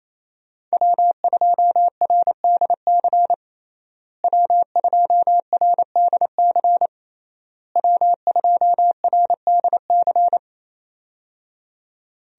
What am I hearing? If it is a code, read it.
Morse code, heard as a keyed tone, about 28 wpm, 700 Hz. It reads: W2RDC W2RDC W2RDC
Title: USA Call Signs - 5 Characters - Repeated 3x - 28wpm